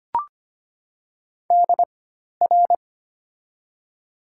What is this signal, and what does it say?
Morse code, heard as a keyed tone, about 25 wpm, spaced as for 12 wpm, 700 Hz. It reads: DR
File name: Top 90 QSO Elements - 4x Character Spacing - 25wpm